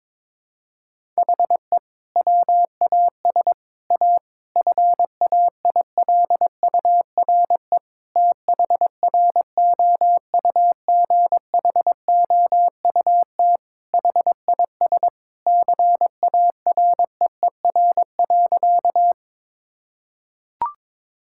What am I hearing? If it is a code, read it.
Morse code, heard as a keyed tone, about 22 wpm, 700 Hz. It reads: HE WAS A FAILURE THROUGHOUT HIS CAREER.
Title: Sentences from Top 1000 Words - Review Part 6 - 22wpm